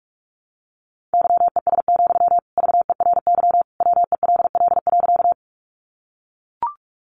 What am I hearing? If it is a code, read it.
Morse code, heard as a keyed tone, about 44 wpm, 700 Hz. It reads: YES, VERY WELL.
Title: Sentences from Top 500 Words 45wpm